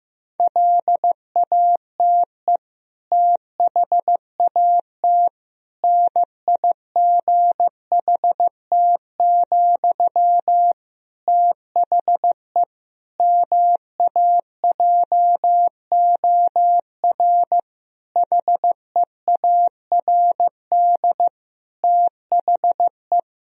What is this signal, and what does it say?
Morse code, heard as a keyed tone, about 15 wpm, 700 Hz. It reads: LATE THAT NIGHT, THE MAJOR HEARD THE
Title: Sentences from Top 1000 Words - Review Part 4 - 15wpm